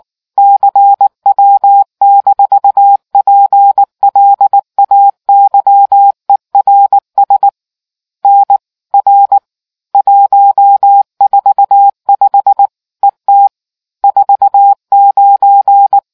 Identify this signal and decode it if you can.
Morse code, heard as a keyed tone, about 19 wpm, 800 Hz. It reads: CW-PLAYERS N R 145 ET 49